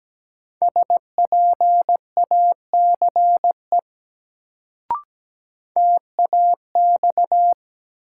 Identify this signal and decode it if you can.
Morse code, heard as a keyed tone, about 17 wpm, 700 Hz. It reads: SPACE TAX
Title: Sentences from Top 1000 Words - Review Part 5 - New Word Drill - 17wpm